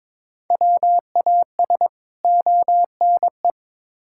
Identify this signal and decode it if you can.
Morse code, heard as a keyed tone, about 22 wpm, 700 Hz. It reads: WAS ONE
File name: practice / 22wpm / Build A Phrase - 5-Word Phrases - 22wpm